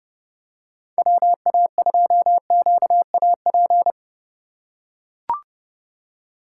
Morse code, heard as a keyed tone, about 30 wpm, 700 Hz. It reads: WA2QAP
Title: USA Call Signs - 6 Characters - 2x Character Spacing - 30wpm